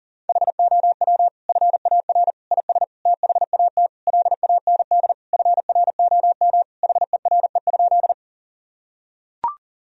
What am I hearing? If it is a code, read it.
Morse code, heard as a keyed tone, about 40 wpm, 700 Hz. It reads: HOW FAR IS THAT LAND FROM HERE?
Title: Sentences from Top 1000 Words - Review Part 1 - 40wpm